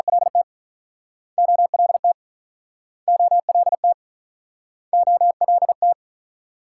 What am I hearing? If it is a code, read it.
Morse code, heard as a keyed tone, about 49 wpm, 700 Hz. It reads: LT OLT OLT OLT